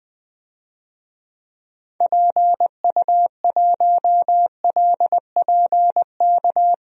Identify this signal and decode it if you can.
Morse code, heard as a keyed tone, about 20 wpm, 700 Hz. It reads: PU1LPK
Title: DXCC-100 Call Signs - 6 Characters - 20wpm